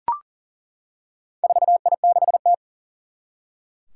Morse code, heard as a keyed tone, about 40 wpm, 700 Hz. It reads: 4I6T
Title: Four Challenging Letter-Number Combinations 40wpm